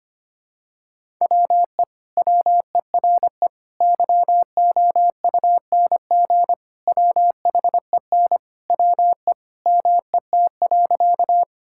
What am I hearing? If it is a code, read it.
Morse code, heard as a keyed tone, about 25 wpm, 700 Hz. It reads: WE WERE YOUNG WHEN WE MET.
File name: practice / 25wpm / Sentences with 2-Word Phrases - 25wpm